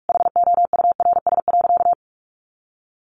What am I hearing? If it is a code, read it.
Morse code, heard as a keyed tone, about 44 wpm, 700 Hz. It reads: HOURS.
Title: Additional Easy Phrases 45wpm